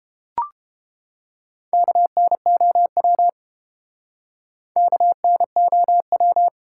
Morse code, heard as a keyed tone, about 32 wpm, 700 Hz. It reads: KNOW KNOW